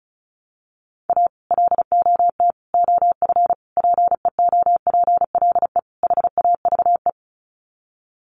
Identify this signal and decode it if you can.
Morse code, heard as a keyed tone, about 34 wpm, 700 Hz. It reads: A LOT OF PEOPLE HAVE
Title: Sets of 5 Words 35wpm